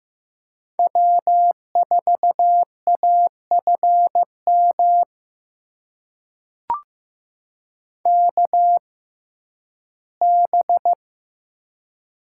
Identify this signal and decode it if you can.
Morse code, heard as a keyed tone, about 15 wpm, spaced as for 11 wpm, 700 Hz. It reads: W4AFM KB
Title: USA Call Signs - 6x Character Spacing - 15wpm